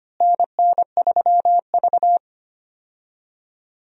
Morse code, heard as a keyed tone, about 25 wpm, 700 Hz. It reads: NN3V
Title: USA Call Signs - 4 Characters - Repeated 2x - 25wpm